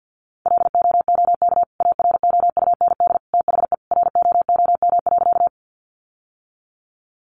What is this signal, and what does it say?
Morse code, heard as a keyed tone, about 50 wpm, 700 Hz. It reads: LOOK AROUND THE ROOM.
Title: Sentences from Top 600 Words 50wpm